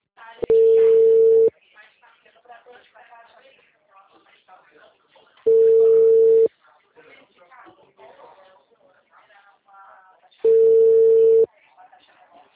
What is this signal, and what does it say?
Ringback tone (China, 450 Hz), with speech.